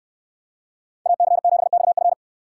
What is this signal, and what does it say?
Morse code, heard as a keyed tone, about 68 wpm, 700 Hz. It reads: AC7XF